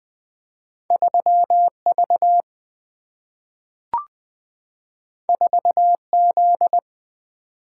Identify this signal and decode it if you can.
Morse code, heard as a keyed tone, about 20 wpm, 700 Hz. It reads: 3 V 4 Z
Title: Single-Letter Single-Number 20wpm